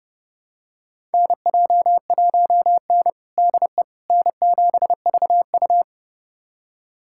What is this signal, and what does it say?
Morse code, heard as a keyed tone, about 32 wpm, 700 Hz. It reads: NJ1N DE N7VU
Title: USA Callsign DE Callsign - 4 Characters - 30wpm